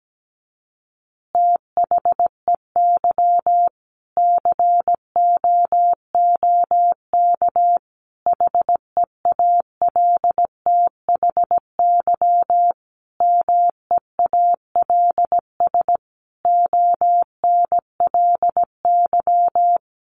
Morse code, heard as a keyed tone, about 17 wpm, 700 Hz. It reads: THEY COOK HEALTHY MEALS ONLY
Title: Easy Phrases 17wpm